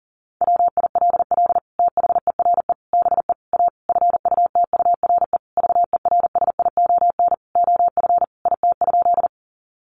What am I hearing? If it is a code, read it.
Morse code, heard as a keyed tone, about 37 wpm, 700 Hz. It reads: WILL THERE BE A FUTURE VERSION OF IT?
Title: Sentences from Top 1400 Words 40wpm